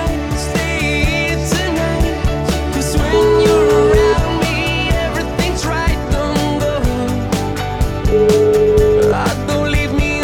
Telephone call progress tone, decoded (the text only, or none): ringback tone